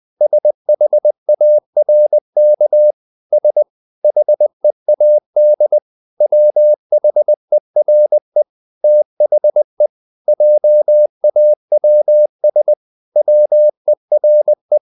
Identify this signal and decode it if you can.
Morse code, heard as a keyed tone, about 20 wpm, 600 Hz. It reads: SHARK S HEAD WHERE THE JAWS WERE
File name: the old man and the sea part 2 20wpm